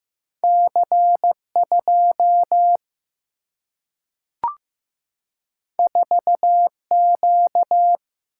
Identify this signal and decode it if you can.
Morse code, heard as a keyed tone, about 15 wpm, 700 Hz. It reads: C 2 4 Q
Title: Single-Letter Single-Number 15wpm